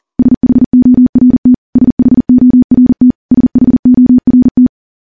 Morse code, heard as a keyed tone, about 40 wpm, 250 Hz. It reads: SHORT SHORT SHORT
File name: morse / morse-training-40WPM